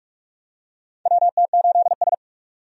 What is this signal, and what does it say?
Morse code, heard as a keyed tone, about 46 wpm, 700 Hz. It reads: WT8S
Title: USA Call Signs - 4 Characters - 45wpm